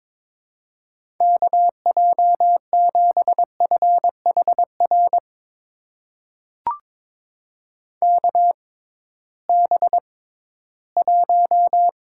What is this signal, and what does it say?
Morse code, heard as a keyed tone, about 22 wpm, spaced as for 14 wpm, 700 Hz. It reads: KJ7FHR KB1
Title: USA Call Signs - 6 Characters - 6x Character Spacing - 22wpm